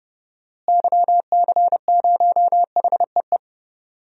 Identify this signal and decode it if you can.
Morse code, heard as a keyed tone, about 31 wpm, 700 Hz. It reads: YC0HEE
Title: DXCC-100 Call Signs - 6 Characters - Repeated 2x - 30wpm